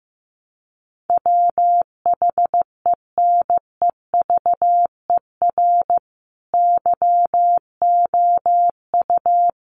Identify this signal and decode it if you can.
Morse code, heard as a keyed tone, about 14 wpm, 700 Hz. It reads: WHENEVER YOU
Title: Sentences from Top 1200 Words 15wpm